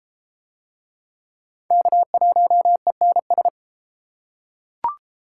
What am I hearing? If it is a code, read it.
Morse code, heard as a keyed tone, about 30 wpm, 700 Hz. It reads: K1ENS